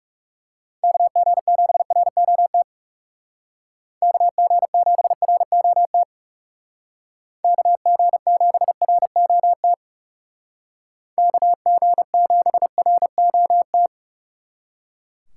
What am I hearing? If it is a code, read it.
Morse code, heard as a keyed tone, about 39 wpm, 700 Hz. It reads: KG7ROT KG7ROT KG7ROT KG7ROT